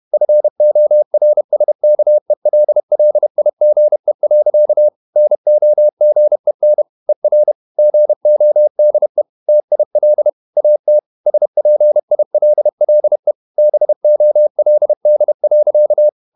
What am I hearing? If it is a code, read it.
Morse code, heard as a keyed tone, about 31 wpm, 600 Hz. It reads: FORSKELLIGE. NOGEN ER GODE TIL AT SPILLE BOLD.
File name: Ferie QRQ 31wpm